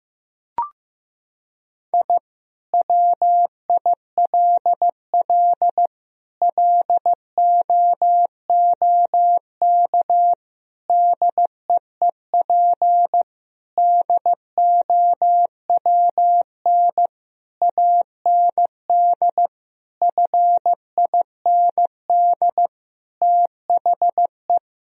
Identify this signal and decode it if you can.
Morse code, heard as a keyed tone, about 15 wpm, 700 Hz. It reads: I WILL LOOK DEEP DOWN AND FIND THE
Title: Sentences from Top 1000 Words - Review Part 3 - 15wpm